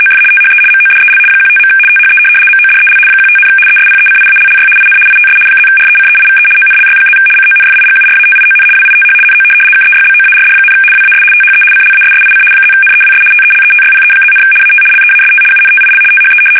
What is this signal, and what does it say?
RTTY, about 75 baud. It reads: OF THE MODEL 15. THE MODEL 15, ESPECIALLY IN ITS +RECEIVE ONLY+ CONFIGURATION WITH NO KEYBOARD, WAS THE CLASSIC +NEWS TELETYPE+, AT LEAST UNTIL THE
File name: fsk850-75B / teletype-corp